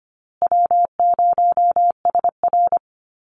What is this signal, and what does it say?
Morse code, heard as a keyed tone, about 25 wpm, 700 Hz. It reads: W0SR
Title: USA Call Signs Repeated 3x 25wpm